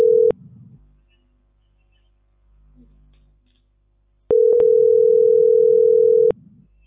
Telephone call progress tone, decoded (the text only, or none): ringback tone